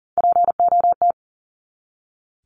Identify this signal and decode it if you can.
Morse code, heard as a keyed tone, about 40 wpm, 700 Hz. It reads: POT
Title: Top 751-1000 Words 40wpm